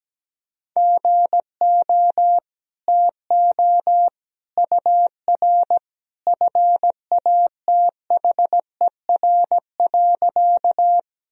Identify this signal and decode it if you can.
Morse code, heard as a keyed tone, about 17 wpm, 700 Hz. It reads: GO TO UR FATHER.
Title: Sentences from Top 300 Words and Abbreviations 17wpm